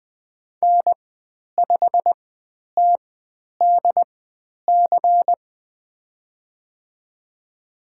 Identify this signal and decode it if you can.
Morse code, heard as a keyed tone, about 20 wpm, spaced as for 10 wpm, 700 Hz. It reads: N5TDC
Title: 109 Lesson 18g - Letter m - Call Signs 20wpm